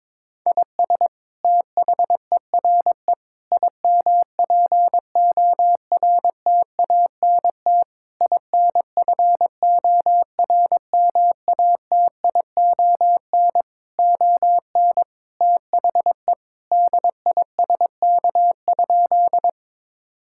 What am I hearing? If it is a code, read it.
Morse code, heard as a keyed tone, about 22 wpm, 700 Hz. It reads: IS THERE IMPORTANT INFORMATION ON THE DISK?
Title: Sentences from Top 1000 Words - Review Part 6 - 22wpm